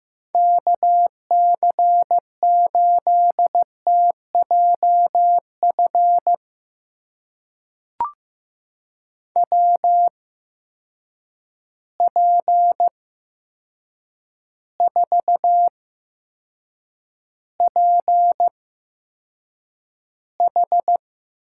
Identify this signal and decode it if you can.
Morse code, heard as a keyed tone, about 15 wpm, spaced as for 7 wpm, 700 Hz. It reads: KC8TJF WP4PH
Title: USA Call Signs - 6 Characters - 8x Character Spacing - 15wpm